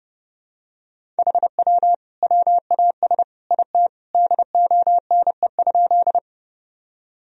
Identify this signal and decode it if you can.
Morse code, heard as a keyed tone, about 30 wpm, 700 Hz. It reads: HW WAS IT DONE?
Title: Sentences from Top 500 Words and Abbreviations 30wpm